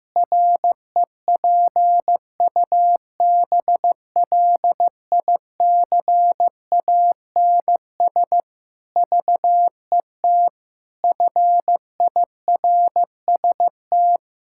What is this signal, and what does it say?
Morse code, heard as a keyed tone, about 15 wpm, 700 Hz. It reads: REPUBLICANS VET FIRST
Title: Pangram Challenge v2 - 15wpm